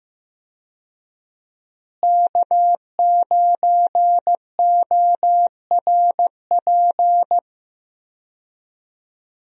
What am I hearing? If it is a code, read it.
Morse code, heard as a keyed tone, about 15 wpm, 700 Hz. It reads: K9ORP
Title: USA Call Signs - 5 Characters - Repeated 2x - 15wpm